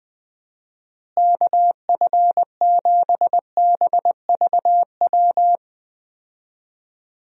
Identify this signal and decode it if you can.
Morse code, heard as a keyed tone, about 21 wpm, 700 Hz. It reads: KF7BVW
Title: USA Call Signs - 6 Characters - Repeated 3x - 20wpm